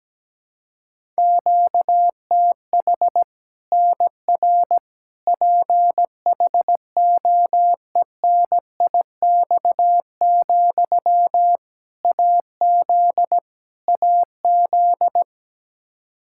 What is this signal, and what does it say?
Morse code, heard as a keyed tone, about 17 wpm, 700 Hz. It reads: QTH NR PHOENIX, AZ AZ
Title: Practice QSO - Exchange 1 - Beginning - 17wpm